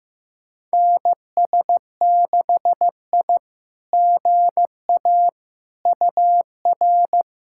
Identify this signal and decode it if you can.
Morse code, heard as a keyed tone, about 15 wpm, 700 Hz. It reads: NS6I GA UR
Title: SOTA QSO Practice 15wpm